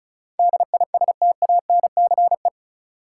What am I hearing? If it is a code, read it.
Morse code, heard as a keyed tone, about 36 wpm, 700 Hz. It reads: DISTANCE
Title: Sentences from Top 1200 Words - New Word Drill Alt - 35wpm